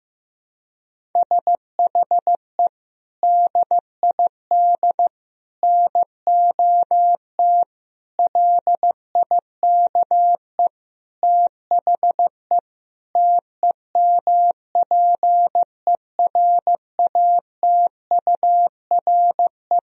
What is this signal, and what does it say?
Morse code, heard as a keyed tone, about 15 wpm, 700 Hz. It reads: SHE DID NOT LIKE THE TEMPERATURE
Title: Sentences from Top 1000 Words - Review Part 4 - 15wpm